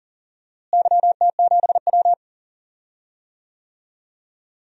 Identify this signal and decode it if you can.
Morse code, heard as a keyed tone, about 40 wpm, 700 Hz. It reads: YT7W
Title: DXCC-100 Call Signs - 4 Characters - Repeated 2x - 40wpm